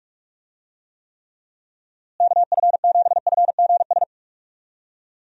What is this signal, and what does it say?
Morse code, heard as a keyed tone, about 44 wpm, 700 Hz. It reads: KF7FGS
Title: USA Call Signs - 6 Characters - Repeated 3x - 45wpm